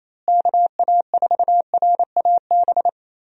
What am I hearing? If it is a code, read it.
Morse code, heard as a keyed tone, about 28 wpm, 700 Hz. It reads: KA4RAB
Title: USA Call Signs - 6 Characters - Repeated 2x - 28wpm